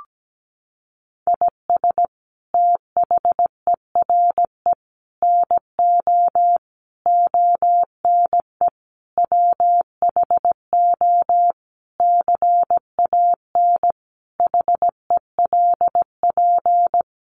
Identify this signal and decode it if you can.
Morse code, heard as a keyed tone, about 17 wpm, 700 Hz. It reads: IS THERE NO ONE WHO CAN HELP
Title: Sentences from Top 200 Words 17wpm